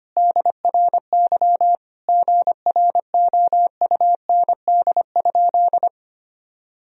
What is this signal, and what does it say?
Morse code, heard as a keyed tone, about 25 wpm, 700 Hz. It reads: DRY GROUND?
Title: Sentences from Top 1000 Words - Review Part 3 - 25wpm